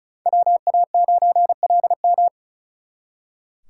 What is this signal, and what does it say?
Morse code, heard as a keyed tone, about 42 wpm, 700 Hz. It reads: WA9LM
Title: Speed Racing - USA 2x2 Call Signs 35wpm